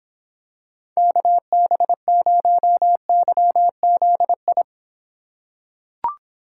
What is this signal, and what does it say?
Morse code, heard as a keyed tone, about 26 wpm, 700 Hz. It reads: KB0YZI